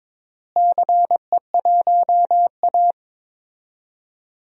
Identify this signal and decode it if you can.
Morse code, heard as a keyed tone, about 23 wpm, 700 Hz. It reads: CE1A
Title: DXCC-100 Call Signs - 4 Characters - Repeated 2x - 22wpm